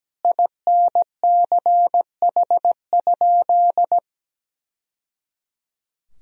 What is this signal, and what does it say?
Morse code, heard as a keyed tone, about 17 wpm, 700 Hz. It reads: INCH?
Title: Sentences from Top 1000 Words - Review Part 3 - 17wpm